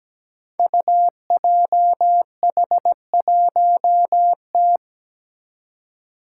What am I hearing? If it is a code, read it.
Morse code, heard as a keyed tone, about 17 wpm, 700 Hz. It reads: UJH1T
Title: Five Challenging Letter-Number Combinations 17wpm